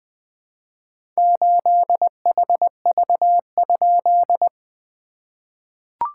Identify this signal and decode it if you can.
Morse code, heard as a keyed tone, about 20 wpm, 700 Hz. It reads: 8HV?
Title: SOTA QSO Practice 20wpm